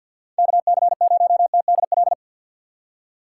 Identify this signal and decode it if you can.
Morse code, heard as a keyed tone, about 50 wpm, 700 Hz. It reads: KC0TDL
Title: USA Call Signs - 6 Characters - 4x Character Spacing - 50wpm